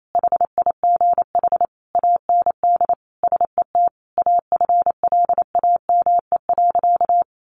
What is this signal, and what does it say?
Morse code, heard as a keyed tone, about 28 wpm, 700 Hz. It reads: HIGH AND SET AFLAME.
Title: A Sound Of Thunder 28wpm